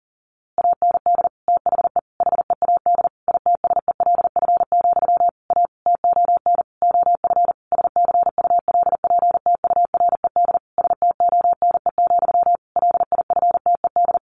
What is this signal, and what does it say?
Morse code, heard as a keyed tone, about 40 wpm, 700 Hz. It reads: AND THE HEAD ITSELF, A TON OF SCULPTURED STONE, LIFTED
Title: A Sound Of Thunder 40wpm